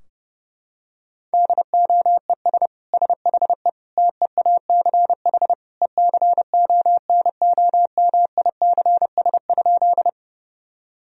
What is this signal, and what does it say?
Morse code, heard as a keyed tone, about 30 wpm, 700 Hz. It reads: DOES SHE TEACH ECONOMICS?
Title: Sentences from Top 1000 Words - Review Part 3 - 30wpm